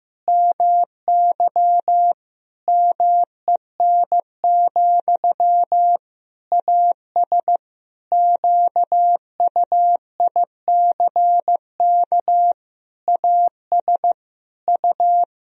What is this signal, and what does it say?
Morse code, heard as a keyed tone, about 15 wpm, 700 Hz. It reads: MY MEN, AS QUICK AS U
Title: Sentences from Top 500 Words and Abbreviations 15wpm